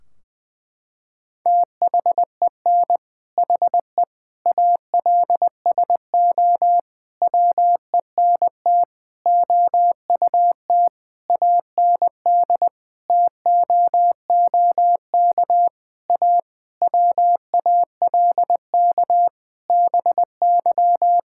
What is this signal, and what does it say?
Morse code, heard as a keyed tone, about 20 wpm, 700 Hz. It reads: THEN HE ALSO WENT OUT AND TOOK A WALK BY
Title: Sentences from Top 1600 Words 20wpm